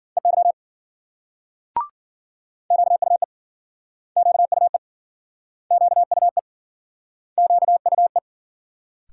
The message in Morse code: EY QUE QUE QUE QUE

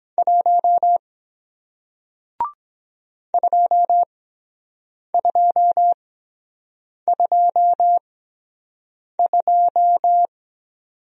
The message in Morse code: 1 2222

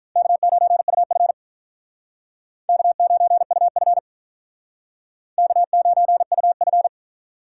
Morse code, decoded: K9UF K9UF K9UF

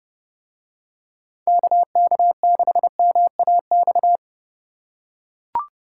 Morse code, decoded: KK6MAX

